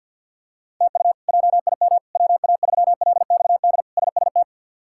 Morse code, heard as a keyed tone, about 53 wpm, 700 Hz. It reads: TU JIM WA3LXD SST
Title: K1USN SST QSO Practice - 50wpm